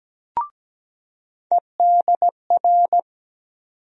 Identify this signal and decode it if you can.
Morse code, heard as a keyed tone, about 17 wpm, 700 Hz. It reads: EDR
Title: USA Call Sign Postfixes - 3 Letters 17wpm